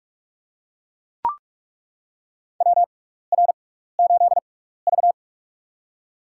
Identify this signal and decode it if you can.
Morse code, heard as a keyed tone, about 41 wpm, spaced as for 15 wpm, 700 Hz. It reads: WR8V
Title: USA Call Signs - 4 Characters - 6x Character Spacing - 45wpm